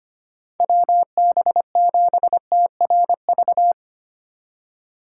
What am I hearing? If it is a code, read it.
Morse code, heard as a keyed tone, about 25 wpm, 700 Hz. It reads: WB7TRV
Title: USA Call Signs - 2x Character Spacing - 25wpm